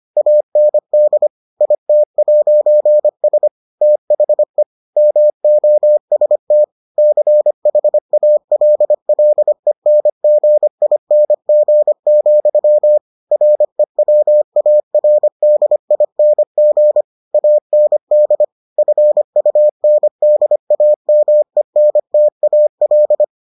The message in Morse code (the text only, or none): AND IT'S THE MOST CHALLENGING, REWARDING AND FUNDAMENTAL